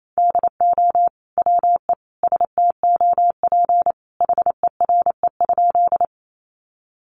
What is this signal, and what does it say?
Morse code, heard as a keyed tone, about 28 wpm, 700 Hz. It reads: DO WE STOP HERE?
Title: Sentences from Top 300 Words 28wpm